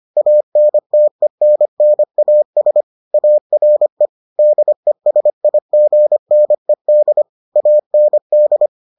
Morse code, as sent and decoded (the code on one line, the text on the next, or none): .- -. - . -. -. .- ... / .- .-. / -.. ... .. --. -. -.. / .- -. -..
ANTENNAS ARE DESIGNED AND